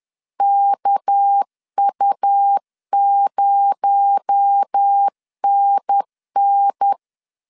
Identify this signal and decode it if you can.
Morse code, heard as a keyed tone, about 10 wpm, 800 Hz. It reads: KU0NN
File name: CW / DET / 1 9 1500C S A 10wpm 4511